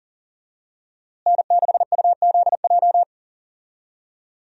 Morse code, heard as a keyed tone, about 40 wpm, 700 Hz. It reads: N6UZJ
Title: USA Call Signs - 5 Characters - Repeated 3x - 40wpm